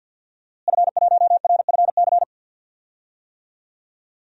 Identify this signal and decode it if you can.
Morse code, heard as a keyed tone, about 52 wpm, 700 Hz. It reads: F1RFC
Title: DXCC-100 Call Signs - 5 Characters - Repeated 3x - 50wpm